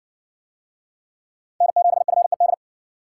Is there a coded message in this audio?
Yes: Morse code from 1.6 to 2.5 s, heard as a keyed tone, about 60 wpm, 700 Hz. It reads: N7FED